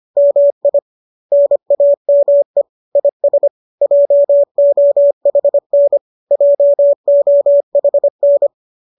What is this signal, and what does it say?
Morse code, heard as a keyed tone, about 25 wpm, 570 Hz. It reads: MI NAME IS JOHN JOHN